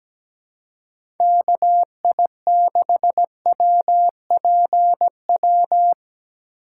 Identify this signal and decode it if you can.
Morse code, heard as a keyed tone, about 17 wpm, 700 Hz. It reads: KI6WPW
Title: USA Call Signs - 3x Character Spacing - 17wpm